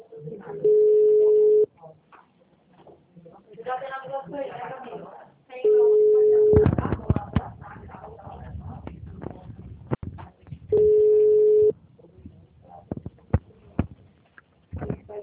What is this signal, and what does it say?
Ringback tone (Europe, 425 Hz), with speech.